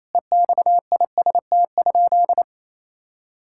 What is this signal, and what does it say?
Morse code, heard as a keyed tone, about 28 wpm, 700 Hz. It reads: EXIST?
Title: Sentences from Top 1000 Words - Review Part 5 - 28wpm